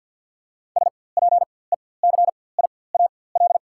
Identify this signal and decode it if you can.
Morse code, heard as a keyed tone, about 47 wpm, spaced as for 23 wpm, 690 Hz. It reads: SPECIAL
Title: 4x Character Spacing - Top 500 Words - 50wpm